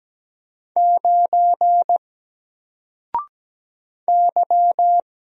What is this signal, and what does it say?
Morse code, heard as a keyed tone, about 17 wpm, 700 Hz. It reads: 9 Y
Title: Single Letter-Number 17wpm